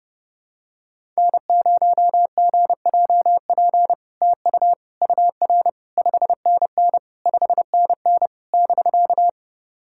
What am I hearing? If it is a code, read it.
Morse code, heard as a keyed tone, about 30 wpm, 700 Hz. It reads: N0GJP TU UR 5NN 5NN <BK>
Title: SOTA QSO Practice 30wpm